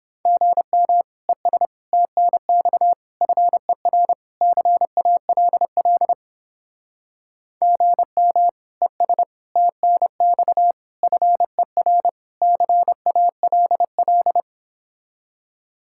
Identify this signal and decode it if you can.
Morse code, heard as a keyed tone, about 31 wpm, 700 Hz. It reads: GM ES TNX FER CALL GM ES TNX FER CALL